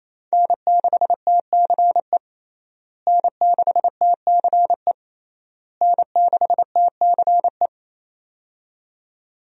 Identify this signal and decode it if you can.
Morse code, heard as a keyed tone, about 28 wpm, 700 Hz. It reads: N6TCE N6TCE N6TCE
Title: USA Call Signs - 5 Characters - Repeated 3x - 28wpm